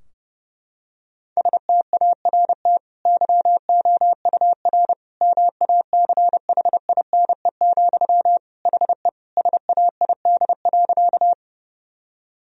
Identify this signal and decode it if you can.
Morse code, heard as a keyed tone, about 30 wpm, 700 Hz. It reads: START YOUR MACHINE, HE SAID.
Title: Sentences from Top 1000 Words - Review Part 2 - 30wpm